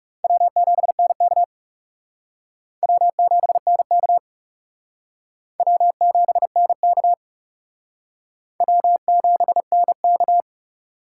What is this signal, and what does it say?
Morse code, heard as a keyed tone, about 37 wpm, 700 Hz. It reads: W7NK W7NK W7NK W7NK
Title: Speed Racing - USA 1x2 Call Signs 30wpm